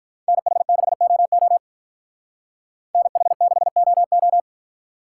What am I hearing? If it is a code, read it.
Morse code, heard as a keyed tone, about 45 wpm, 700 Hz. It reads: NH6OO NH6OO